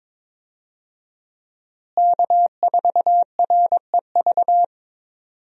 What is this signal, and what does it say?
Morse code, heard as a keyed tone, about 22 wpm, 700 Hz. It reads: K4REV
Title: USA Call Signs - 5 Characters - Repeated 2x - 22wpm